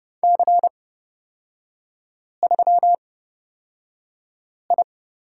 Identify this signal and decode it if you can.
Morse code, heard as a keyed tone, about 30 wpm, spaced as for 5 wpm, 700 Hz. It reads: C3I
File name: Lesson 37e - Letter x - Three Characters 30wpm